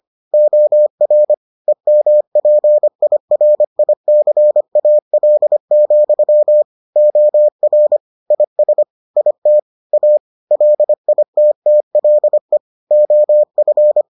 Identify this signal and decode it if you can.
Morse code, heard as a keyed tone, about 25 wpm, 600 Hz. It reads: OR EMPIRICAL, OR IS IT A LITTLE OF